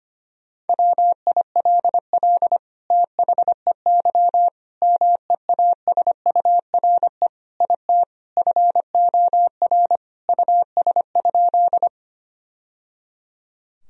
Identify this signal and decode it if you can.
Morse code, heard as a keyed tone, about 25 wpm, 700 Hz. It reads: WILL THEY MEASURE IT FOR US?
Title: Sentences from Top 1000 Words - Review Part 2 - 25wpm